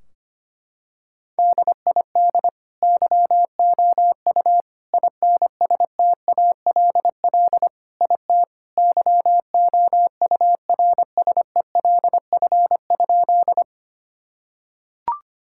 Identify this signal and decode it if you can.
Morse code, heard as a keyed tone, about 25 wpm, 700 Hz. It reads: DID YOU INSTALL IT YOURSELF?
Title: Sentences from Top 1600 Words 25wpm